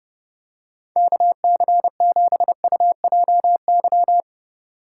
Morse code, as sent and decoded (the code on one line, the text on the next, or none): -.- -.-. --... ..- .--- -.--
KC7UJY